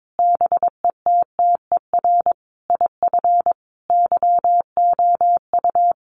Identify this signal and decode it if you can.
Morse code, heard as a keyed tone, about 22 wpm, 700 Hz. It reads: BETTER IF YOU
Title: Sentences from Top 400 Words 22wpm